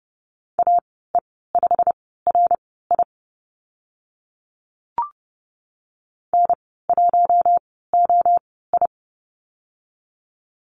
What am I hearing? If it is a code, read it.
Morse code, heard as a keyed tone, about 30 wpm, spaced as for 17 wpm, 700 Hz. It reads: AE5RI N1OI
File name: Lesson 15g - Number 5 - Call Signs 30wpm